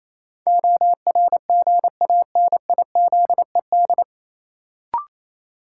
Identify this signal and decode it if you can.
Morse code, heard as a keyed tone, about 28 wpm, 700 Hz. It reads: ORGANIZED
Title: Sentences from Top 1300 Words - New Word Drill Alt - 28wpm